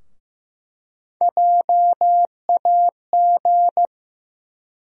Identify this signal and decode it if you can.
Morse code, heard as a keyed tone, about 15 wpm, 700 Hz. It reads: JAG
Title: USA Call Sign Postfixes - 3 Letters 15wpm